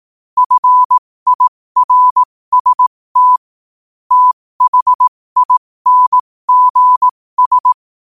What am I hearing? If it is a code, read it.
Morse code, heard as a keyed tone, about 18 wpm, spaced as for 15 wpm, 1000 Hz. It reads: FIRST THINGS
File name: qotdfiles / 15wpm